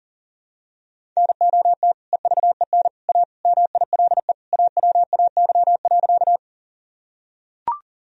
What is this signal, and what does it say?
Morse code, heard as a keyed tone, about 40 wpm, 700 Hz. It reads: NOT EVEN A MILE AWAY.
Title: Sentences from Top 1000 Words - Review Part 1 - 40wpm